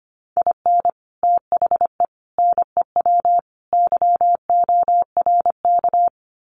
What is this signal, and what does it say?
Morse code, heard as a keyed tone, about 25 wpm, 700 Hz. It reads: IN THE NEW YORK
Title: Sets of 4 Words 25wpm